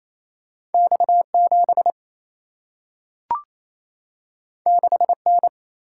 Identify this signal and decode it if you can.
Morse code, heard as a keyed tone, about 28 wpm, 700 Hz. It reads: X 7 6 N
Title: Single-Letter Single-Number 28wpm